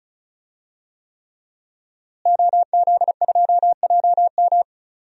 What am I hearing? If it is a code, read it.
Morse code, heard as a keyed tone, about 34 wpm, 700 Hz. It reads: OZ2JM
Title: DXCC-100 Call Signs - 5 Characters - Repeated 3x - 35wpm